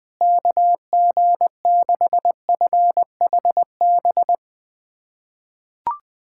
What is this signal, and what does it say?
Morse code, heard as a keyed tone, about 21 wpm, 700 Hz. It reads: KG6FHB
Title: USA Callsign DE Callsign - 6 Characters - 20wpm